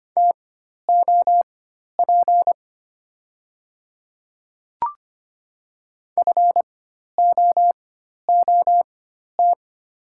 Morse code, heard as a keyed tone, about 25 wpm, spaced as for 12 wpm, 700 Hz. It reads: TOP FOOT
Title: 4x Character Spacing - Top 500 Words - 25wpm